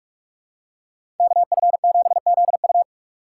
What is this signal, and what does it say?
Morse code, heard as a keyed tone, about 47 wpm, 700 Hz. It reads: KF7ZU
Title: USA Call Signs - 5 Characters - 2x Character Spacing - 45wpm